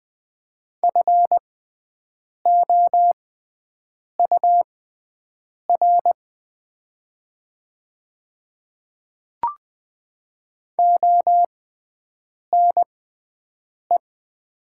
Morse code, heard as a keyed tone, about 20 wpm, spaced as for 7 wpm, 700 Hz. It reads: FOUR ONE
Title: 6x Character Spacing - Top 300 Words - 20wpm